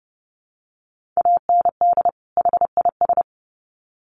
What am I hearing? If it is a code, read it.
Morse code, heard as a keyed tone, about 30 wpm, 700 Hz. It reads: AND HIS
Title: Sets of 2 Words 30wpm